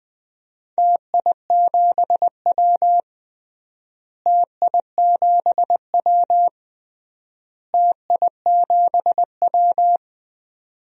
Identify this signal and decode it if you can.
Morse code, heard as a keyed tone, about 21 wpm, 700 Hz. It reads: TI7W TI7W TI7W
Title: DXCC-100 Call Signs - 4 Characters - Repeated 3x - 20wpm